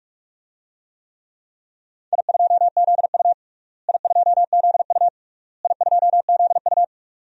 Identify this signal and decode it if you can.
Morse code, heard as a keyed tone, about 45 wpm, 700 Hz. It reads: I2ZU I2ZU I2ZU
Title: DXCC-100 Call Signs - 4 Characters - Repeated 3x - 45wpm